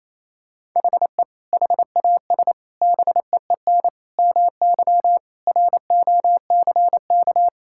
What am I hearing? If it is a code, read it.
Morse code, heard as a keyed tone, about 27 wpm, 700 Hz. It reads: HE HAS BEEN MY ROCK
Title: Sentences from Top 1000 Words - Review Part 5 - 28wpm